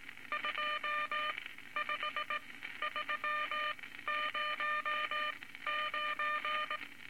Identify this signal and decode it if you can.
Morse code, heard as a keyed tone, about 19 wpm, spaced as for 13 wpm, 1200 Hz, with noise. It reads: 25309
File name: Morse Code Records / 19 A LCM 166 Ten Words Per Minute